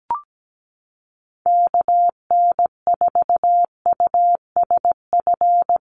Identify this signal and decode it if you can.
Morse code, heard as a keyed tone, about 17 wpm, 700 Hz. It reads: KN4USF
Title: USA Call Signs Repeated 2x 17wpm